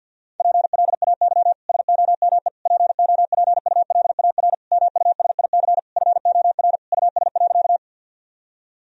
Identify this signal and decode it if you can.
Morse code, heard as a keyed tone, about 50 wpm, 700 Hz. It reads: PLAY SOME POPULAR MUSIC FOR US.
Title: Sentences from Top 1000 Words - Review Part 3 - 50wpm